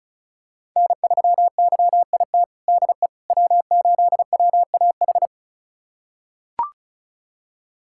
Morse code, heard as a keyed tone, about 36 wpm, 700 Hz. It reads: N3YIT DE W8WAH
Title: USA Callsign DE Callsign - 5 Characters - 35wpm